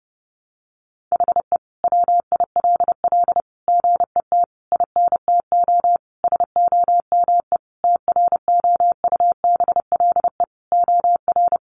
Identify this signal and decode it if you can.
Morse code, heard as a keyed tone, about 30 wpm, 700 Hz. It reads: HE WILL GET INTO SOME TROUBLE OR
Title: Sentences from Top 1500 Words 30wpm